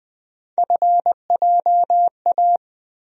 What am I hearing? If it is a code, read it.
Morse code, heard as a keyed tone, about 20 wpm, 700 Hz. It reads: FJA
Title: USA Call Sign Postfixes - 3 Letters 20wpm